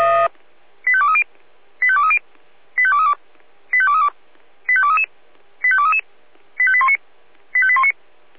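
Five-tone selective call (ZVEI), heard as quick runs of five tones, tones of about 70 ms. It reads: 85230 85230 85232 85232 85233 85233 85819 85819
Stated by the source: ZVEI five-tone selective call received on an SDR.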